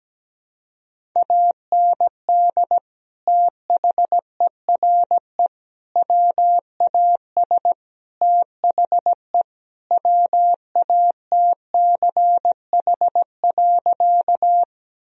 Morse code, heard as a keyed tone, about 17 wpm, 700 Hz. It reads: AND THERE WAS THE WATCH.
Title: Sentences from Top 1000 Words - Review Part 1 - 17wpm